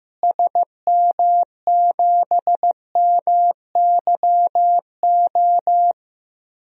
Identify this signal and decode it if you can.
Morse code, heard as a keyed tone, about 15 wpm, 700 Hz. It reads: SM7MYO
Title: DXCC-100 Call Signs - 6 Characters - Repeated 2x - 15wpm